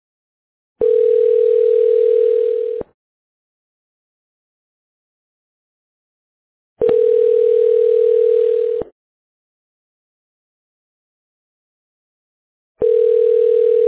Ringback tone (North America), 440 + 480 Hz.